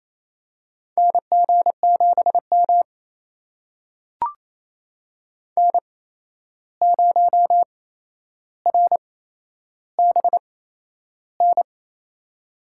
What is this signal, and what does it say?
Morse code, heard as a keyed tone, about 28 wpm, spaced as for 11 wpm, 700 Hz. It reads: NG7M N0RBN
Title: USA Call Signs - 8x Character Spacing - 28wpm